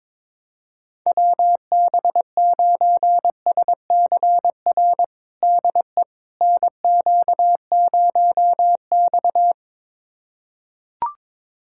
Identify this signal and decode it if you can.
Morse code, heard as a keyed tone, about 22 wpm, 700 Hz. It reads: WB9SCR DE NQ0X
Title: USA Callsign DE Callsign 22wpm